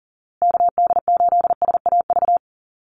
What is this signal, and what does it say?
Morse code, heard as a keyed tone, about 40 wpm, 700 Hz. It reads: KD8SAV
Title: Call Signs Repeated 2x - 40wpm